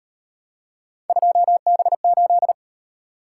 Morse code, heard as a keyed tone, about 38 wpm, 700 Hz. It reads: 2B8